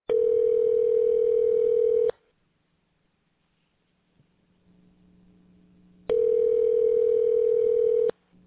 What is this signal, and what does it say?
Ringback tone (North America), 440 + 480 Hz.